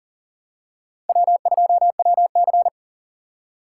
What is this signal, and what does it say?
Morse code, heard as a keyed tone, about 40 wpm, 700 Hz. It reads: W2WC
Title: USA Call Signs - 4 Characters - 6x Character Spacing - 40wpm